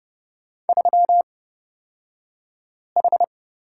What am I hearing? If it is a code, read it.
Morse code, heard as a keyed tone, about 31 wpm, spaced as for 5 wpm, 700 Hz. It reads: 3H
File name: Lesson 38d - Letter q - Two Characters 30wpm